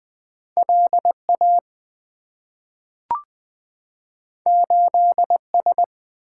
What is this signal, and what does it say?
Morse code, heard as a keyed tone, about 20 wpm, 700 Hz. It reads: LA 8S
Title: Two Letter-Number Combinations - 20wpm